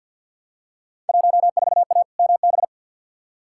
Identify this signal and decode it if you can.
Morse code, heard as a keyed tone, about 46 wpm, 700 Hz. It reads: 14A MB
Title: Field Day Exchanges Encore - 50wpm